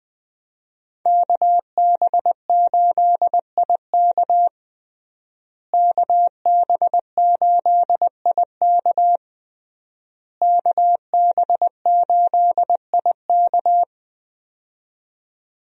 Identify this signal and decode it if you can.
Morse code, heard as a keyed tone, about 20 wpm, 700 Hz. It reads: KB8IK KB8IK KB8IK
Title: USA Call Signs - 5 Characters - Repeated 3x - 20wpm